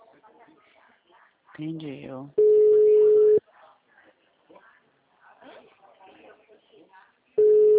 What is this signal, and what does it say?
Ringback tone (Europe, 425 Hz), with speech.